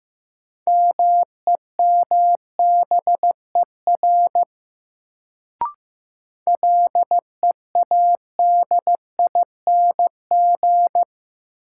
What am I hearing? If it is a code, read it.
Morse code, heard as a keyed tone, about 15 wpm, 700 Hz. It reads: MEMBER LEADING
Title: Sentences from Top 1100 Words - New Word Drill Alt - 15wpm